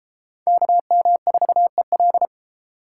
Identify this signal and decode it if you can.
Morse code, heard as a keyed tone, about 30 wpm, 700 Hz. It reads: KM4EL